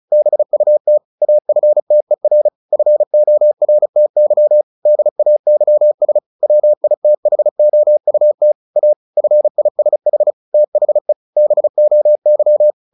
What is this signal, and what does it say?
Morse code, heard as a keyed tone, about 34 wpm, 600 Hz. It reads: BUT AFTER FORTY DAYS WITHOUT A FISH THE BOY
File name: the old man and the sea / the old man and the sea 1 35wpm